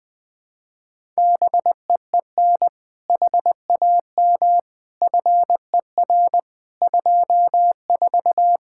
Morse code, heard as a keyed tone, about 21 wpm, 700 Hz. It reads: BEEN HAM FER 24
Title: Practice QSO - Exchange 3 - More Info - 20wpm